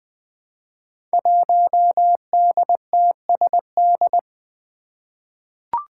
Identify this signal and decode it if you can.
Morse code, heard as a keyed tone, about 20 wpm, 700 Hz. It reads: 1DTSD
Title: Five Challenging Letter-Number Combinations 20wpm